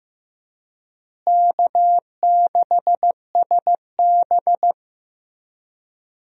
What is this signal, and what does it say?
Morse code, heard as a keyed tone, about 15 wpm, 700 Hz. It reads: K6SB
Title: USA Call Signs - 4 Characters - Repeated 3x - 15wpm